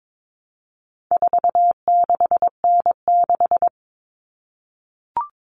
Morse code, heard as a keyed tone, about 22 wpm, 700 Hz. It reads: 46N6
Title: Common Cut Numbers - 4-Digit 22wpm